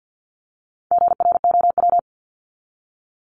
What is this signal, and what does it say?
Morse code, heard as a keyed tone, about 50 wpm, 700 Hz. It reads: GROW
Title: ING Suffix - Top 100 Words with Base Words 50wpm